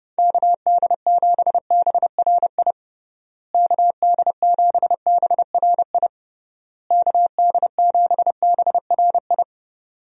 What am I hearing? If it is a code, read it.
Morse code, heard as a keyed tone, about 30 wpm, 700 Hz. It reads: KD7BRI KD7BRI KD7BRI
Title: USA Call Signs - 6 Characters - Repeated 3x - 30wpm